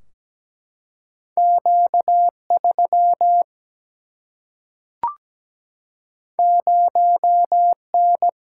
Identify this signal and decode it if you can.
Morse code, heard as a keyed tone, about 17 wpm, 700 Hz. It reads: Q3 0N
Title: Single-Letter Single-Number 17wpm